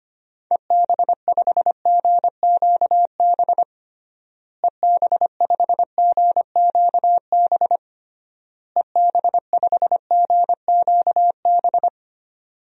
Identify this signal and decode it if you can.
Morse code, heard as a keyed tone, about 25 wpm, 700 Hz. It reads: EB5GQB EB5GQB EB5GQB
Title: DXCC-100 Call Signs - 6 Characters - Repeated 3x - 25wpm